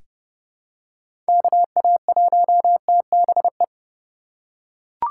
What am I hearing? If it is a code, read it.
Morse code, heard as a keyed tone, about 30 wpm, 700 Hz. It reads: KA1TBE
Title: USA Call Signs - 6 Characters - 2x Character Spacing - 30wpm